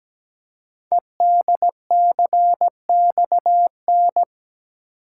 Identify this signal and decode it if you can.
Morse code, heard as a keyed tone, about 17 wpm, 700 Hz. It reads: EDCXN
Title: Code Groups - 5 Letters - 17wpm